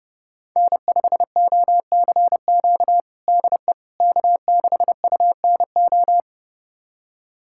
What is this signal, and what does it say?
Morse code, heard as a keyed tone, about 32 wpm, 700 Hz. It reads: N5OCQ DE K6UNO
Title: USA Callsign DE Callsign - 5 Characters - 30wpm